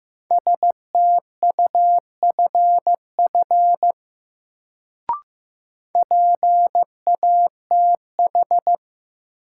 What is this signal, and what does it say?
Morse code, heard as a keyed tone, about 15 wpm, 700 Hz. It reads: STUFF PATH
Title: Sentences from Top 1400 Words - New Word Drill Alt - 15wpm